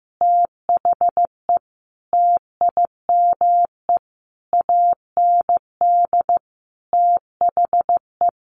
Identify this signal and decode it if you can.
Morse code, heard as a keyed tone, about 15 wpm, 700 Hz. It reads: THE TIME AND THE
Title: Sentences from Top 1500 Words 15wpm